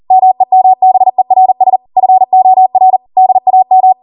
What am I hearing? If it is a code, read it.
Morse code, heard as a keyed tone, about 40 wpm, 750 Hz. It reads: MEMBERS FOR DAM